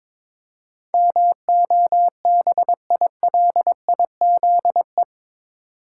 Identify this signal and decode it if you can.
Morse code, heard as a keyed tone, about 22 wpm, 700 Hz. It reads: MOBILIZE
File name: practice / 22wpm / Eight Letter Words - 22wpm